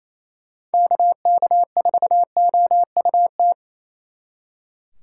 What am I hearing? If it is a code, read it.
Morse code, heard as a keyed tone, about 33 wpm, 700 Hz. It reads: KK4OUT
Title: Speed Racing - USA 2x3 Call Signs 28wpm